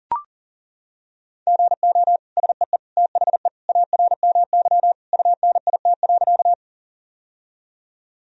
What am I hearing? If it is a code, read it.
Morse code, heard as a keyed tone, about 37 wpm, 700 Hz. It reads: GO SEE THE ARMY UNIT.
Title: Sentences from Top 1000 Words - Review Part 2 - 40wpm